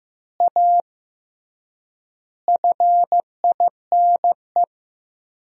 Morse code, heard as a keyed tone, about 15 wpm, 700 Hz. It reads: A FINE
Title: 3x Word-Spacing - Sentences from Top 500 Words - 15wpm